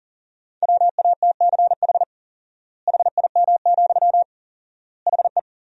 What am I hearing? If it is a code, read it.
Morse code, heard as a keyed tone, about 37 wpm, 700 Hz. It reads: WATCH HIM, HE
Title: 4x Word-Spacing - Sentences from Top 300 Words - 40wpm